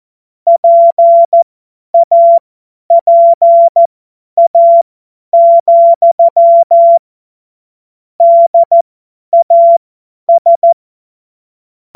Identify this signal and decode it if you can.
Morse code, heard as a keyed tone, about 14 wpm, spaced as for 10 wpm, 680 Hz. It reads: PAPA, DAS
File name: Morsekurs / Witze / Witz 02. Februar 2021, 50 F70